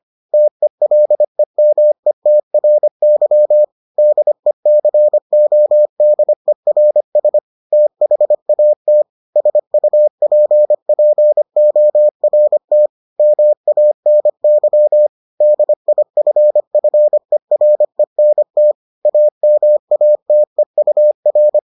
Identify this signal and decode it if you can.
Morse code, heard as a keyed tone, about 25 wpm, 600 Hz. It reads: TELEMETRY DECODERS THAT SUPPORT MANY DIFFERENT AMATEUR